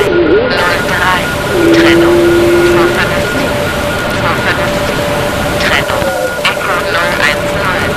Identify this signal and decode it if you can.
Two-tone teleprinter signal, about 850 Hz shift, with speech.
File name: GC / GC69PX5